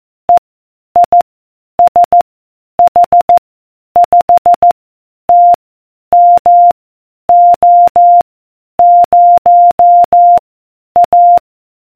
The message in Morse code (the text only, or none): E I S H 5 T M O 0 A